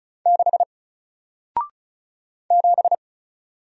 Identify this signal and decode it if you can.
Morse code, heard as a keyed tone, about 36 wpm, 700 Hz. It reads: 6 7
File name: Single-Letter-Number-35wpm